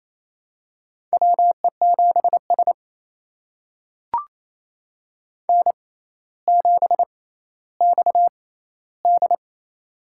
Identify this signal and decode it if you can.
Morse code, heard as a keyed tone, about 28 wpm, spaced as for 14 wpm, 700 Hz. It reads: WE7S N7XD
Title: USA Call Signs - 4 Characters - 6x Character Spacing - 28wpm